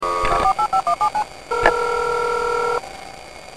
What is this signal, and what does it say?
DTMF, heard as telephone touch tones, tones of about 90 ms. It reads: 48517
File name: Sound-of-a-56k-modem